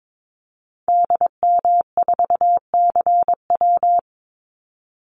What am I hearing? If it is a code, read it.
Morse code, heard as a keyed tone, about 22 wpm, 700 Hz. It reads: DM4CW
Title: Call Signs - 22wpm